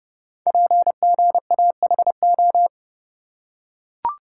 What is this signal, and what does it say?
Morse code, heard as a keyed tone, about 30 wpm, 700 Hz. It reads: PGAHO
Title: Code Groups - 5 Letters - 30wpm